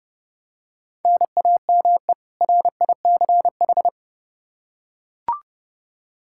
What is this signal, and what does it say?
Morse code, heard as a keyed tone, about 30 wpm, 700 Hz. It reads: NAME RICH
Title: K1USN SST QSO Practice 30wpm